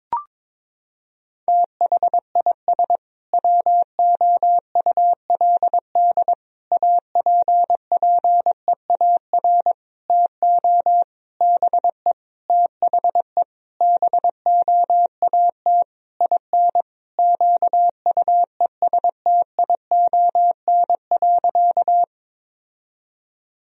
Morse code, heard as a keyed tone, about 22 wpm, 700 Hz. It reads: THIS WOULD APPEAR TO BE THE BOAT IN QUESTION.
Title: Sentences from Top 1000 Words - Review Part 3 - 22wpm